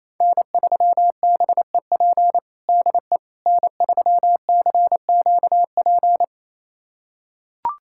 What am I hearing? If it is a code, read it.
Morse code, heard as a keyed tone, about 28 wpm, 700 Hz. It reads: N3BEP DE N3CQP
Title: USA Callsign DE Callsign - 5 Characters - 28wpm